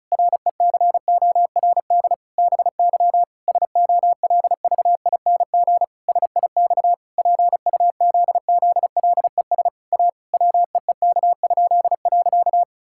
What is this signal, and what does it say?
Morse code, heard as a keyed tone, about 36 wpm, 700 Hz. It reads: RECORD BY SOLVING SIX PUZZLES A WEEK?.
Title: Pangram Challenge v2 - 35wpm